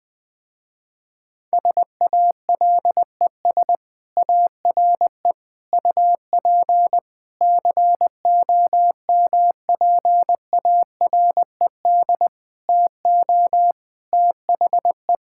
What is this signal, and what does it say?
Morse code, heard as a keyed tone, about 21 wpm, 700 Hz. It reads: SALES ARE UP COMPARED TO THE
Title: Sentences with 7-Word Phrases - Spoken Beginning - 20wpm